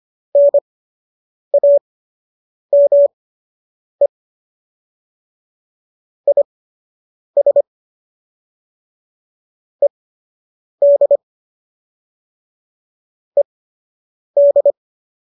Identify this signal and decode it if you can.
Morse code, heard as a keyed tone, about 25 wpm, spaced as for 8 wpm, 570 Hz. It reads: NAME IS ED ED
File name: qso003 08